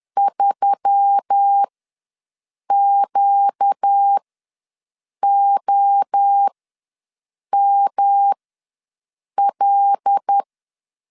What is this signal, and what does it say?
Morse code, heard as a keyed tone, about 10 wpm, spaced as for 6 wpm, 800 Hz. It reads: 3QOML
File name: LEZIONI / 5 LBVC / 5 LBV ws 10wpm 3645